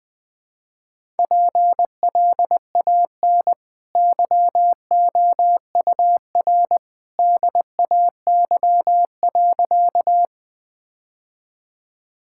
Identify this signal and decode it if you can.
Morse code, heard as a keyed tone, about 20 wpm, 700 Hz. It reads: PLAN YOUR DAY.
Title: Easy Sentences with Hints - 20wpm